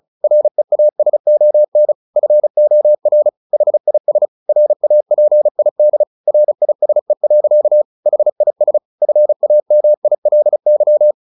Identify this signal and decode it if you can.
Morse code, heard as a keyed tone, about 34 wpm, 600 Hz. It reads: REASON FOR HIS RAPID RISE. HIS FAMILY